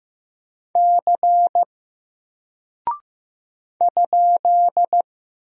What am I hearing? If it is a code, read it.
Morse code, heard as a keyed tone, about 15 wpm, 700 Hz. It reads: C ?
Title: Single Character (Letter:Number:Punctuation:Prosign) 15wpm